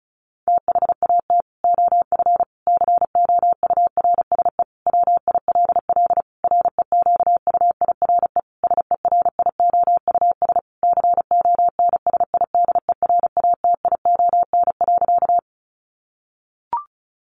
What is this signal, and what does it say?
Morse code, heard as a keyed tone, about 34 wpm, 700 Hz. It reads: THAT OF COURSE WILL REQUIRE SERIOUS CONSIDERATION.
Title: Sentences from Top 1500 Words 35wpm